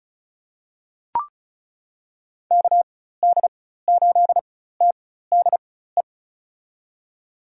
Morse code, heard as a keyed tone, about 36 wpm, spaced as for 16 wpm, 700 Hz. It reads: KD8TDE
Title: USA Call Signs - 4x Character Spacing - 35wpm